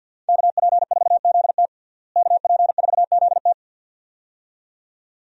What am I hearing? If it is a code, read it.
Morse code, heard as a keyed tone, about 51 wpm, 700 Hz. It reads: KP4ZT KP4ZT
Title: USA Call Signs - 5 Characters - Repeated 2x - 50wpm